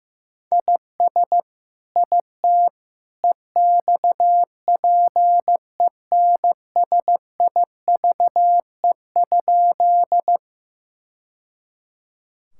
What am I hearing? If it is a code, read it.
Morse code, heard as a keyed tone, about 16 wpm, 700 Hz. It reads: IS IT EXPENSIVE?
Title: Sentences from Top 1000 Words - Review Part 6 - 15wpm